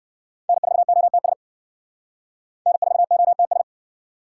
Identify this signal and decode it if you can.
Morse code, heard as a keyed tone, about 57 wpm, 700 Hz. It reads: N4OTS N4OTS
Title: Speed Racing - USA 1x3 Call Signs 50wpm